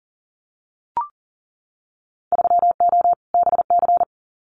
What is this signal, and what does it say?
Morse code, heard as a keyed tone, about 40 wpm, 700 Hz. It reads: 3O BC
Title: Winter Field Day Exchanges Encore 40wpm